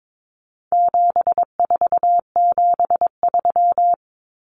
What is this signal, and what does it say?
Morse code, heard as a keyed tone, about 22 wpm, 700 Hz. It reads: 7473
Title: Common Cut Numbers - 4-Digit 22wpm